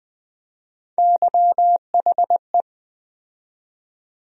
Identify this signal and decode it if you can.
Morse code, heard as a keyed tone, about 20 wpm, 700 Hz. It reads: YHE
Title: USA Call Sign Postfixes - 3 Letters 20wpm